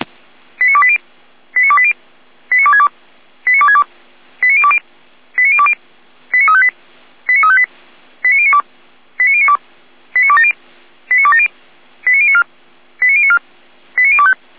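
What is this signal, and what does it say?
Five-tone selective call (ZVEI), heard as quick runs of five tones, tones of about 70 ms. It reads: 89280 89280 89262 89262 89020 89020 89358 89358 89092 89092 89270 89270 89094 89094 89025
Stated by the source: Five-tone ZVEI selective call received on SDR.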